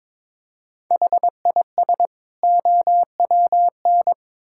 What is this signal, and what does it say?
Morse code, heard as a keyed tone, about 23 wpm, 700 Hz. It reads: HIS OWN
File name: practice / 22wpm / Build A Phrase - 2-Word Phrases - 22wpm